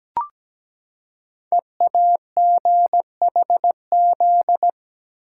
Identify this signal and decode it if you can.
Morse code, heard as a keyed tone, about 17 wpm, 700 Hz. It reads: EAGHZ
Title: Code Groups - 5 Letters - 17wpm